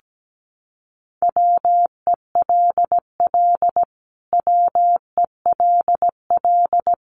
Morse code, heard as a keyed tone, about 17 wpm, 700 Hz. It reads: WELL WELL
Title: Top 500 Words Repeated 17wpm